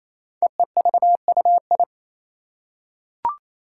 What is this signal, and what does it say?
Morse code, heard as a keyed tone, about 27 wpm, 700 Hz. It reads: EEVUI
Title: Code Groups - 5 Letters - 28wpm